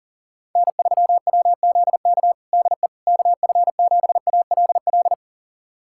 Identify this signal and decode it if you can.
Morse code, heard as a keyed tone, about 40 wpm, 700 Hz. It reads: N3WZK DE KF7ALL
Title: USA Callsign DE Callsign 40wpm